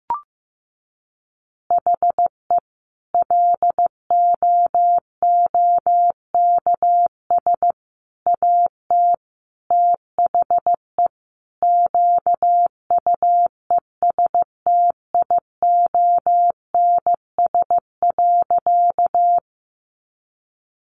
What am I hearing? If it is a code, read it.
Morse code, heard as a keyed tone, about 15 wpm, 700 Hz. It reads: HE LOOKS AT THE QUESTIONS.
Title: Additional Easy Phrases 15wpm